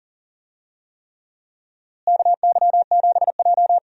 Morse code, heard as a keyed tone, about 39 wpm, 700 Hz. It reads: KY7J
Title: USA Call Signs - 4 Characters - Repeated 3x - 40wpm